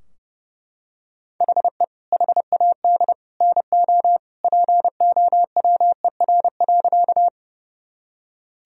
Morse code, heard as a keyed tone, about 30 wpm, 700 Hz. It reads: HE HAD NO POWER.
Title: Sentences from Top 1000 Words - Review Part 2 - 30wpm